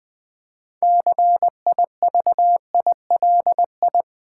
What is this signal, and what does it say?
Morse code, heard as a keyed tone, about 20 wpm, 700 Hz. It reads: CIVILI
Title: Build A Word - 6 Letters At A Time - 12-Letter Words - 20wpm